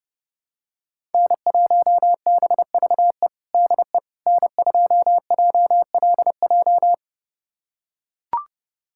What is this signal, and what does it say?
Morse code, heard as a keyed tone, about 32 wpm, 700 Hz. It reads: N1BVE DE N2JLJ
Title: USA Callsign DE Callsign - 5 Characters - 30wpm